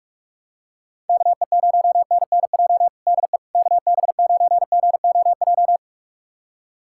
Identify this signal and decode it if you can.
Morse code, heard as a keyed tone, about 46 wpm, 690 Hz. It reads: KE0NNJ DE KB9GOJ
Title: USA Callsign DE Callsign - 6 Characters - 45wpm